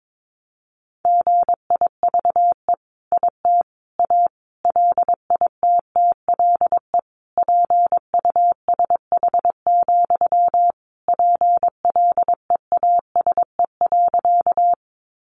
Morse code, heard as a keyed tone, about 22 wpm, 700 Hz. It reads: GIVE IT A LITTLE PUSH, PLEASE.
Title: Sentences from Top 1500 Words 22wpm